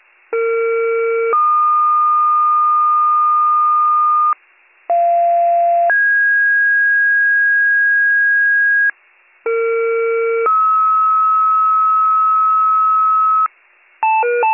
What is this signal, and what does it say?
Two-tone page, heard as a one-second tone, then a three-second tone, about 460 then 1220 Hz, about 690 then 1650 Hz, about 460 then 1290 Hz. It is heard from 0.3 to 13.5 s.